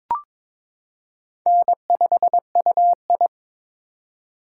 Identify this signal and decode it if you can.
Morse code, heard as a keyed tone, about 22 wpm, 700 Hz. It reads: N5UI
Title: K1USN SST QSO Practice - 22wpm